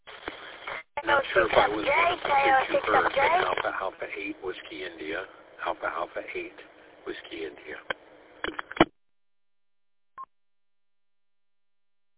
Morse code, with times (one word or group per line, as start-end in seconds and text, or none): none